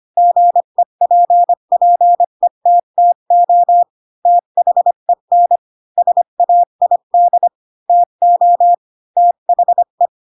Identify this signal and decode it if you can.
Morse code, heard as a keyed tone, about 25 wpm, 700 Hz. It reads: GEPPETTO THEN SAID TO THE
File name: Pinocchio / Pinocchio ch8 20 25 ik1hge